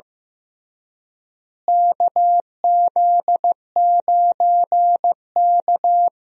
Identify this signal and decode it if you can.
Morse code, heard as a keyed tone, about 15 wpm, 700 Hz. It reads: KZ9K
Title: USA Call Signs - 4 Characters - Repeated 3x - 15wpm